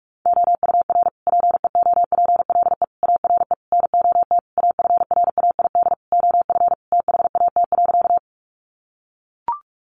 Morse code, heard as a keyed tone, about 44 wpm, 700 Hz. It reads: OUR PEOPLE ARE NOT AFRAID OF THAT.
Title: Sentences from Top 1500 Words 45wpm